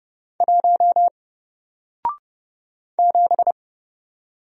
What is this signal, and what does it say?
Morse code, heard as a keyed tone, about 30 wpm, 700 Hz. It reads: 1 7
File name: Single-Letter-Number-30wpm